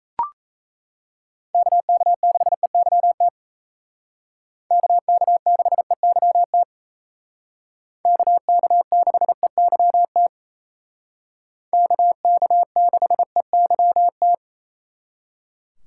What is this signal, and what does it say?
Morse code, heard as a keyed tone, about 36 wpm, 700 Hz. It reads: KK6EYT KK6EYT KK6EYT KK6EYT